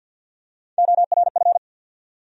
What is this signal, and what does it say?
Morse code, heard as a keyed tone, about 49 wpm, 700 Hz. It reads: ORF
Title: USA Call Sign Postfixes - 3 Letters 50wpm